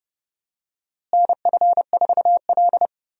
Morse code, heard as a keyed tone, about 30 wpm, 700 Hz. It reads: NF4L
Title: USA Call Signs - 4 Characters - Repeated 3x - 30wpm